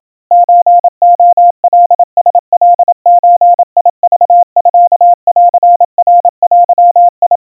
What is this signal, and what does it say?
Morse code (Japanese Wabun code), heard as a keyed tone, about 27 wpm, 700 Hz. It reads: ソレカラカゾクミンナデ
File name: kawa / Morse sample / kamyu 100 cpm 700 Hz Crispy